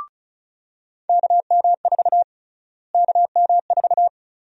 Morse code, heard as a keyed tone, about 36 wpm, 700 Hz. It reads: KM4 KM4
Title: USA Call Sign Prefixes - Repeated 2x - 35wpm